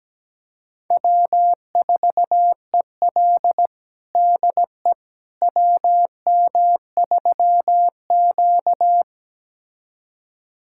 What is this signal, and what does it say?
Morse code, heard as a keyed tone, about 17 wpm, 700 Hz. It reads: W4EL DE WM3Q
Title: USA Callsign DE Callsign - 4 Characters - 17wpm